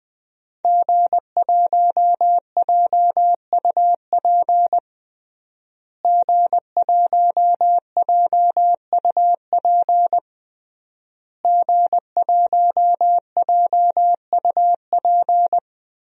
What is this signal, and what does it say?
Morse code, heard as a keyed tone, about 21 wpm, 700 Hz. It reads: G1JUP G1JUP G1JUP
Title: DXCC-100 Call Signs - 5 Characters - Repeated 3x - 20wpm